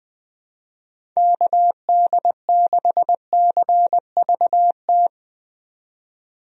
Morse code, heard as a keyed tone, about 20 wpm, 700 Hz. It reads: KD6CVT
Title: USA Call Signs - 6 Characters - 20wpm